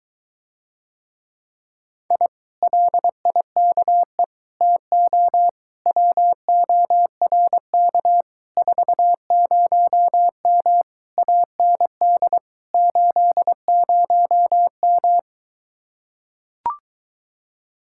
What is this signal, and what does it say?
Morse code, heard as a keyed tone, about 23 wpm, 700 Hz. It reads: I LIKE TO WORK 40M AND 80M